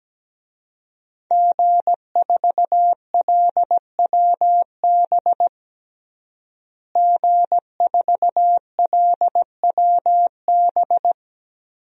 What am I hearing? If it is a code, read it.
Morse code, heard as a keyed tone, about 17 wpm, 700 Hz. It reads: G4LWB G4LWB
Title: DXCC-100 Call Signs - 5 Characters - Repeated 3x - 17wpm